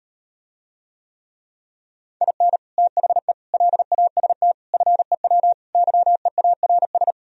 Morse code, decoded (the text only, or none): IN THE LAST FEW YEARS